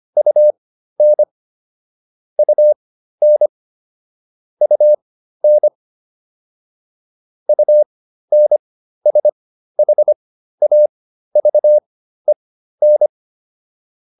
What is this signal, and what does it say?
Morse code, heard as a keyed tone, about 25 wpm, spaced as for 13 wpm, 600 Hz. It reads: UN UN UN UNSHAVEN